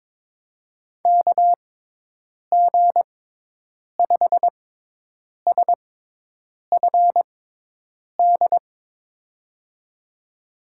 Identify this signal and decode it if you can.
Morse code, heard as a keyed tone, about 22 wpm, spaced as for 8 wpm, 700 Hz. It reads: KG5SFD
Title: USA Call Signs - 6 Characters - 6x Character Spacing - 22wpm